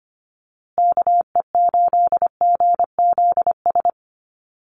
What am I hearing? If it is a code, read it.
Morse code, heard as a keyed tone, about 25 wpm, 700 Hz. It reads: KE8GZS
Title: USA Call Signs Repeated 3x 25wpm